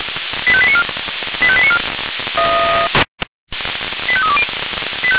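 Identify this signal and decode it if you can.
Five-tone selective call (ZVEI), heard as quick runs of five tones, tones of about 80 ms. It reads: 85894 85894 85320
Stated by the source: ZVEI five-tone alert received by SDR.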